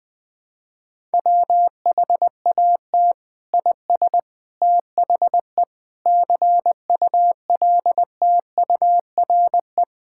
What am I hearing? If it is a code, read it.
Morse code, heard as a keyed tone, about 20 wpm, 700 Hz. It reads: WHAT IS THE CULTURE
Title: Sentences from Top 1000 Words - Review Part 6 - 20wpm